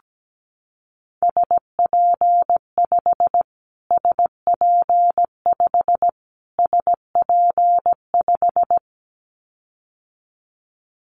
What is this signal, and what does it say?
Morse code, heard as a keyed tone, about 17 wpm, 700 Hz. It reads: SP5 SP5 SP5
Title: Callsign Prefixes for Top 25 DXCC - Repeated 3x - 17wpm